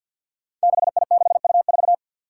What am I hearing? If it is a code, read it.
Morse code, heard as a keyed tone, about 52 wpm, 700 Hz. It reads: 6I6U4
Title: Five Challenging Letter-Number Combinations 50wpm